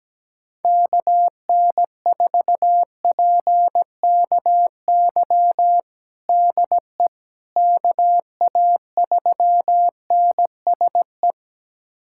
Morse code, heard as a keyed tone, about 17 wpm, 700 Hz. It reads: KN4PKY DE KA3NSE
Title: USA Callsign DE Callsign - 6 Characters - 17wpm